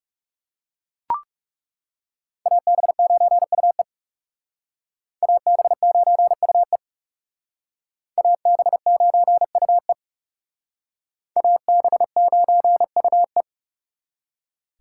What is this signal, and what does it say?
Morse code, heard as a keyed tone, about 36 wpm, 700 Hz. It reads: AB9UE AB9UE AB9UE AB9UE
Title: Speed Racing - USA 2x2 Call Signs 30wpm